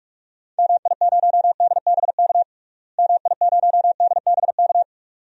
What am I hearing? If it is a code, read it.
Morse code, heard as a keyed tone, about 41 wpm, 700 Hz. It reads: MI0DBK MI0DBK
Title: DXCC-100 Call Signs - 6 Characters - Repeated 2x - 45wpm